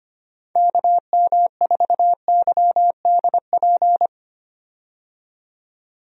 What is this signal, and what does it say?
Morse code, heard as a keyed tone, about 25 wpm, 700 Hz. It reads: KM4YDP
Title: USA Call Signs - 6 Characters - 25wpm